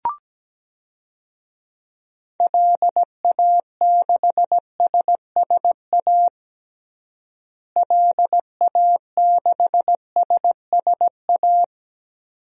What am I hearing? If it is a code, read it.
Morse code, heard as a keyed tone, about 17 wpm, 700 Hz. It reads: LA6SSA LA6SSA
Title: DXCC-100 Call Signs - 6 Characters - Repeated 2x - 17wpm